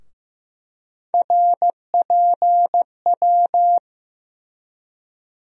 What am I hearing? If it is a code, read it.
Morse code, heard as a keyed tone, about 15 wpm, 700 Hz. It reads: RPW
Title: USA Call Sign Postfixes - 3 Letters 15wpm